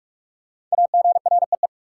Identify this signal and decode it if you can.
Morse code, heard as a keyed tone, about 41 wpm, 700 Hz. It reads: AGREE
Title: Sets of 5 Words Encore - New Word Drill - 45wpm